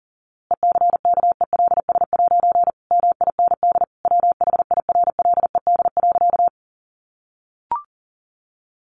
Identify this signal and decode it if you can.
Morse code, heard as a keyed tone, about 40 wpm, 700 Hz. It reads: ECKELS' MIND WHIRLED.
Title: A Sound Of Thunder 40wpm